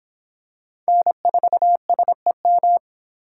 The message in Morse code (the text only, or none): N4SEM